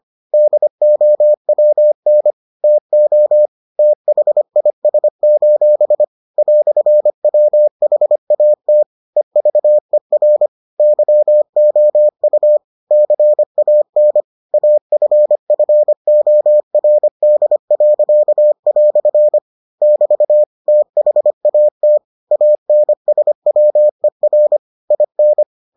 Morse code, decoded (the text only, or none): DOWN TO THIS: "WHAT EVER YOU CAN AFFORD." <BT> THAT ANSWER IN